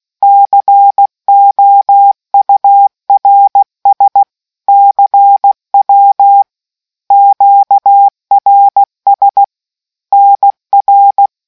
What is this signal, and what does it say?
Morse code, heard as a keyed tone, about 16 wpm, 800 Hz. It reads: COURS CW QRS NR